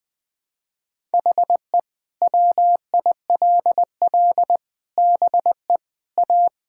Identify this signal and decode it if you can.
Morse code, heard as a keyed tone, about 21 wpm, 700 Hz. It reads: HE WILL BE A
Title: Sentences from Top 1000 Words - Review Part 5 - 20wpm